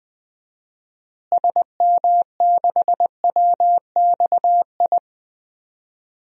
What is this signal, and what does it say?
Morse code, heard as a keyed tone, about 20 wpm, 700 Hz. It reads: SM6WXI
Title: DXCC-100 Call Signs - 6 Characters - Repeated 2x - 20wpm